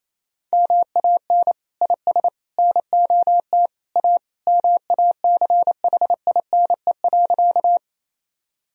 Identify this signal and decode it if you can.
Morse code, heard as a keyed tone, about 28 wpm, 700 Hz. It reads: MAN IS NOT A MACHINE.
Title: Sentences from Top 1000 Words - Review Part 2 - 28wpm